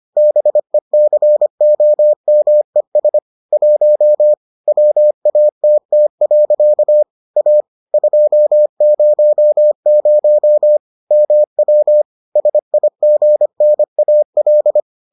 Morse code, heard as a keyed tone, about 25 wpm, 600 Hz. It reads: BECOMES 1 WATT. A 200 MW SIGNAL